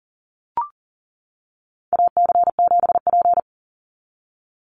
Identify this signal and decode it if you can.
Morse code, heard as a keyed tone, about 40 wpm, 700 Hz. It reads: AC7P
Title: NA SOTA Chaser Callsigns 40wpm